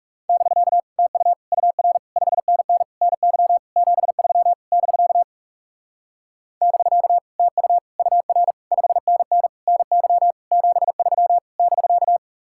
Morse code, heard as a keyed tone, about 38 wpm, 700 Hz. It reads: <BK> TU UR 5NN NY 73 <BK> <BK> TU UR 5NN NY 73 <BK>